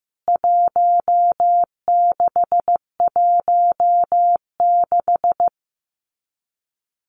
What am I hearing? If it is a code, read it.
Morse code, heard as a keyed tone, about 15 wpm, 700 Hz. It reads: 1616
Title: CWT Exchange 15wpm